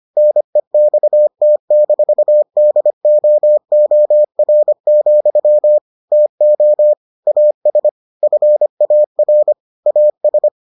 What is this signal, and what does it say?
Morse code, heard as a keyed tone, about 25 wpm, 600 Hz. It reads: NEXT-DOOR, TO AS FAR AS